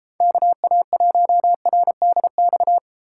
Morse code, heard as a keyed tone, about 33 wpm, 700 Hz. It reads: KA1RDX